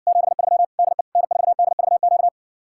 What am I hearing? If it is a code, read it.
Morse code, heard as a keyed tone, about 60 wpm, 700 Hz. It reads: 73 DE N4DVQ